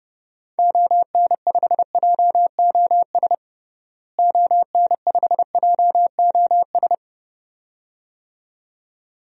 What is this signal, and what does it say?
Morse code, heard as a keyed tone, about 32 wpm, 700 Hz. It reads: ON5JOS ON5JOS
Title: DXCC-100 Call Signs - 6 Characters - Repeated 2x - 30wpm